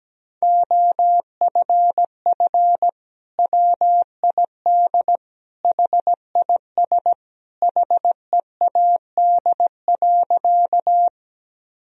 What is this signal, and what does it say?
Morse code, heard as a keyed tone, about 17 wpm, 700 Hz. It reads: OFF WID HIS HEAD.
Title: Sentences from Top 300 Words and Abbreviations 17wpm